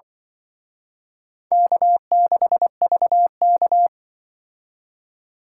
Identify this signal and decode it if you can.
Morse code, heard as a keyed tone, about 23 wpm, 700 Hz. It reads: K6VK
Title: Speed Racing - USA 1x2 Call Signs 20wpm